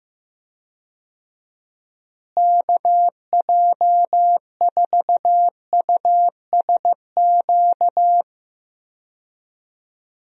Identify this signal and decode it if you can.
Morse code, heard as a keyed tone, about 16 wpm, 700 Hz. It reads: KJ4USQ
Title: USA Call Signs - 6 Characters - Repeated 2x - 15wpm